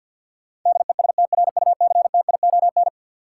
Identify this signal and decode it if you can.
Morse code, heard as a keyed tone, about 49 wpm, 700 Hz. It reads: DESTRUCTION
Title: Sets of 5 Words Encore - New Word Drill - 50wpm